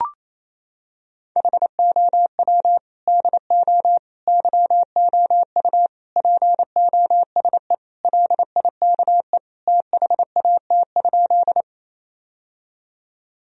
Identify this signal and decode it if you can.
Morse code, heard as a keyed tone, about 27 wpm, 700 Hz. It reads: HOW DO YOU POSE LIKE THAT?
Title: Sentences from Top 1000 Words - Review Part 2 - 28wpm